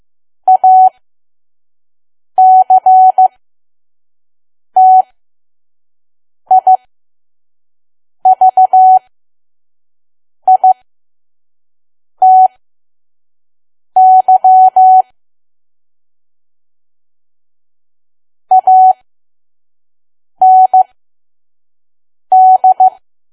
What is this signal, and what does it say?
Morse code, heard as a keyed tone, about 15 wpm, spaced as for 5 wpm, 750 Hz. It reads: ACTIVITY AND
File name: Archive / 5 WPM files / PM